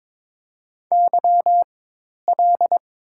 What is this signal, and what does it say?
Morse code, heard as a keyed tone, about 22 wpm, spaced as for 10 wpm, 700 Hz. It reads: YL
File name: Top 90 QSO Elements - 4x Character Spacing - 22wpm